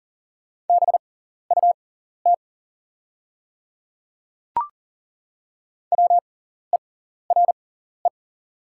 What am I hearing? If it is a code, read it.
Morse code, heard as a keyed tone, about 39 wpm, spaced as for 14 wpm, 700 Hz. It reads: BUT WERE
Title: 6x Character Spacing - Top 100 Words - 40wpm